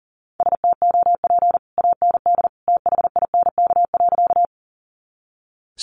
Morse code, heard as a keyed tone, about 40 wpm, 700 Hz. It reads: STOP AND THINK.
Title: Sentences from Top 300 Words 40wpm